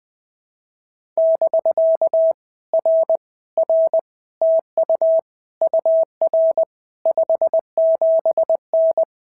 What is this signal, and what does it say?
Morse code, heard as a keyed tone, about 19 wpm, 650 Hz. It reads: <BK> R R TU UR 57N
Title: POTA QSO Practice - 20wpm